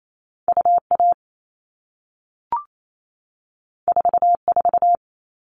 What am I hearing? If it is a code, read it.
Morse code, heard as a keyed tone, about 28 wpm, 700 Hz. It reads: UA 44
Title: All Cut Numbers - 2-Digit 28wpm